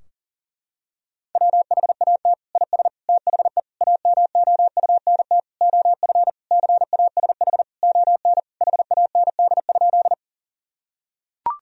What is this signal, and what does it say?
Morse code, heard as a keyed tone, about 37 wpm, 700 Hz. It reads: WHAT IS THE AMOUNT OF CASH ON HAND?
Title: Sentences from Top 1000 Words - Review Part 4 - 40wpm